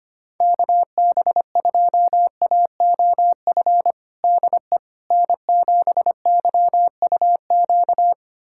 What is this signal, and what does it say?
Morse code, heard as a keyed tone, about 25 wpm, 700 Hz. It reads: KB2AOF DE N7YUQ
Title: USA Callsign DE Callsign 25wpm